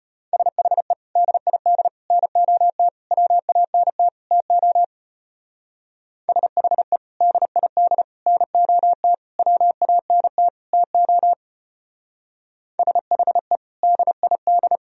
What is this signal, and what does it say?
Morse code, heard as a keyed tone, about 32 wpm, 700 Hz. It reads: SHE DID NOT WANT TO SHE DID NOT WANT TO SHE DID